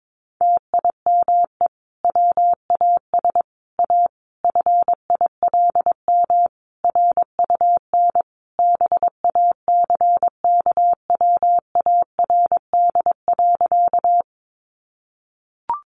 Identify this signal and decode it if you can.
Morse code, heard as a keyed tone, about 22 wpm, 700 Hz. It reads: TIME WAS A FILM RUN BACKWARD.
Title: A Sound Of Thunder 22wpm